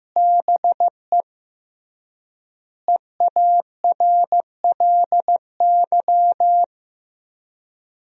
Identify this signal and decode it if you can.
Morse code, heard as a keyed tone, about 14 wpm, 700 Hz. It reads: BE EARLY
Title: 3x Word-Spacing - Sentences from Top 400 Words - 15wpm